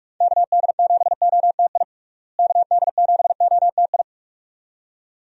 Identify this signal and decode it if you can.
Morse code, heard as a keyed tone, about 46 wpm, 700 Hz. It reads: KD7OTI KD7OTI
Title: USA Call Signs - 6 Characters - Repeated 2x - 45wpm